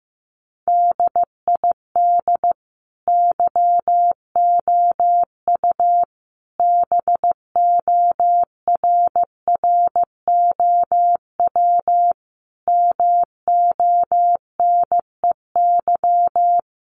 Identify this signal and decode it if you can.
Morse code, heard as a keyed tone, about 15 wpm, 700 Hz. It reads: DID YOU BORROW MONEY
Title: Sentences from Top 1500 Words 15wpm